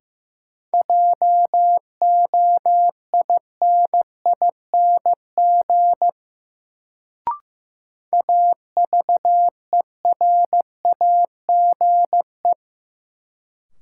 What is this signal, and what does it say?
Morse code, heard as a keyed tone, about 15 wpm, 700 Hz. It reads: JOINING AVERAGE
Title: Sets of 5 Words - New Word Drill - 15wpm